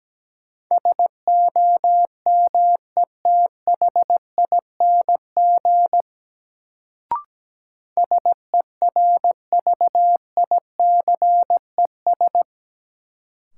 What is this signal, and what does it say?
Morse code, heard as a keyed tone, about 17 wpm, 700 Hz. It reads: SOMETHING SERVICES
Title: Sets of 5 Words - New Word Drill - 17wpm